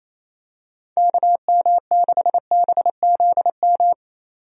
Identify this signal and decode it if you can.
Morse code, heard as a keyed tone, about 28 wpm, 700 Hz. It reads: KM6BZM
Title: USA Call Signs - 6 Characters - 28wpm